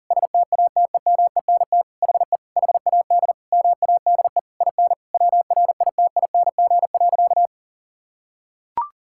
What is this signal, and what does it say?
Morse code, heard as a keyed tone, about 40 wpm, 700 Hz. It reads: STATEMENT HE HAD MADE IN WRITING.
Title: Sentences from Top 1600 Words 40wpm